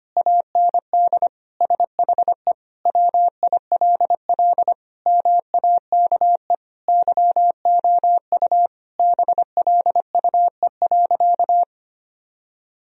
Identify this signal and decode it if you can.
Morse code, heard as a keyed tone, about 25 wpm, 700 Hz. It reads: AND SHE WILL MAKE YOU BLUE.
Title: Sentences from Top 1000 Words - Review Part 3 - 25wpm